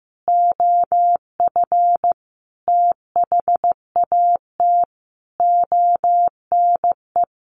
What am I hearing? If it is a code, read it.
Morse code, heard as a keyed tone, about 15 wpm, 700 Hz. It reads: OF THAT ONE
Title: A Sound Of Thunder 15wpm